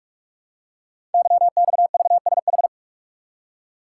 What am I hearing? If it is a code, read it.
Morse code, heard as a keyed tone, about 44 wpm, 700 Hz. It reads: YXVSH
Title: Code Groups - 5 Letter:Numbers - 45wpm